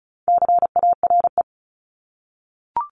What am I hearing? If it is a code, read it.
Morse code, heard as a keyed tone, about 36 wpm, 700 Hz. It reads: CARE
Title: Multi-letter Phonemes 35wpm